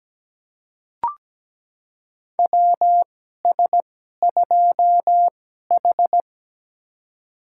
Morse code, heard as a keyed tone, about 17 wpm, spaced as for 12 wpm, 700 Hz. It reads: WS2H
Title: USA Call Signs - 4 Characters - 2x Character Spacing - 17wpm